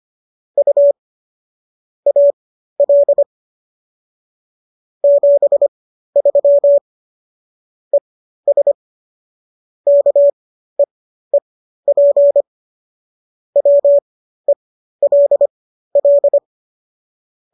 Morse code, heard as a keyed tone, about 25 wpm, spaced as for 13 wpm, 570 Hz. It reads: U AL 73 ES KEEP WELL